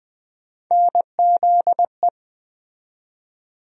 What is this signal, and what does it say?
Morse code, heard as a keyed tone, about 20 wpm, 700 Hz. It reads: NZE
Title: USA Call Sign Postfixes - 3 Letters 20wpm